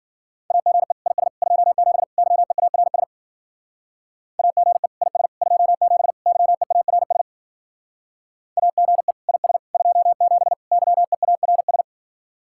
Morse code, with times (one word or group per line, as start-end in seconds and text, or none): 0.5-0.9 AGE
1.1-1.3 IS
1.4-2.0 27
2.2-3.0 YEARS
4.4-4.9 AGE
5.0-5.3 IS
5.4-6.1 27
6.3-7.2 YEARS
8.6-9.1 AGE
9.3-9.6 IS
9.7-10.5 27
10.7-11.8 YEARS